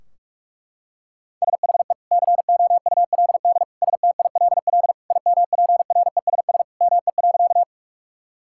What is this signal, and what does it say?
Morse code, heard as a keyed tone, about 46 wpm, 700 Hz. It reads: SHE COULD STILL IMPRESS ME.
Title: Sentences from Top 1600 Words 45wpm